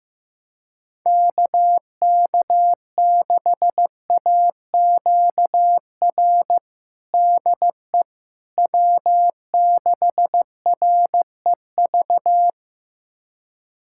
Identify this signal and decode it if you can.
Morse code, heard as a keyed tone, about 16 wpm, 700 Hz. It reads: KK6AQR DE W6REV
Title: USA Callsign DE Callsign 15wpm